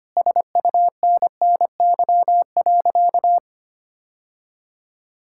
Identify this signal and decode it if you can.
Morse code, heard as a keyed tone, about 25 wpm, 700 Hz. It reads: SUNNY.
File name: practice / 25wpm / Sentences with 3-Word Phrases - 25wpm